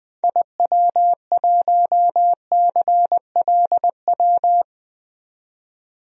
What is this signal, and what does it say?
Morse code, heard as a keyed tone, about 20 wpm, 700 Hz. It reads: IW1CLW
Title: DXCC-100 Call Signs - 6 Characters - 20wpm